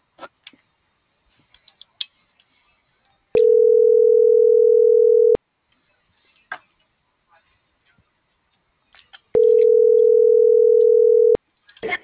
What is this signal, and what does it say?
Ringback tone (North America), 440 + 480 Hz.